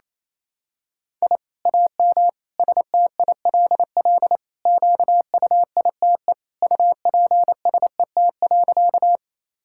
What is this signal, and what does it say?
Morse code, heard as a keyed tone, about 28 wpm, 700 Hz. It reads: I AM STILL QUITE UPSET.
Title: Sentences from Top 1000 Words - Review Part 5 - 28wpm